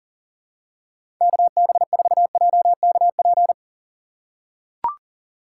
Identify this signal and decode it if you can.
Morse code, heard as a keyed tone, about 40 wpm, 700 Hz. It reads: KB4JKP
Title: USA Call Signs - 6 Characters - 3x Character Spacing - 40wpm